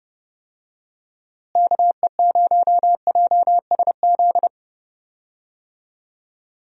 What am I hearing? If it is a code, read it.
Morse code, heard as a keyed tone, about 30 wpm, 700 Hz. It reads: KE0JSZ
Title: USA Call Signs - 6 Characters - Repeated 2x - 30wpm